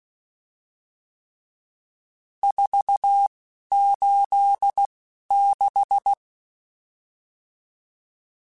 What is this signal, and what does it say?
Morse code, heard as a keyed tone, about 16 wpm, spaced as for 11 wpm, 800 Hz. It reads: 486
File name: RSTs / RST-mp3s / Various RST @10